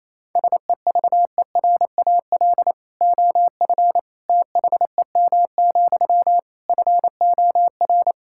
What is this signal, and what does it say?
Morse code, heard as a keyed tone, about 28 wpm, 700 Hz. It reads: SEVERAL OF THEM, FOR
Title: Sentences from Top 1000 Words - Review Part 5 - 28wpm